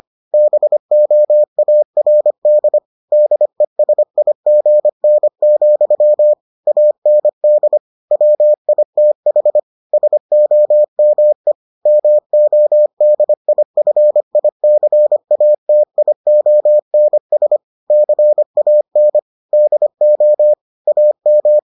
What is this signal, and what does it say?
Morse code, heard as a keyed tone, about 25 wpm, 600 Hz. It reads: BOARD DESIGN, AND WITH SOME MODIFICATIONS CAN DO AM